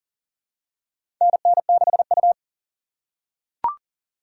Morse code, heard as a keyed tone, about 40 wpm, 700 Hz. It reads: NN6U
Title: USA Call Signs - 4 Characters - 3x Character Spacing - 40wpm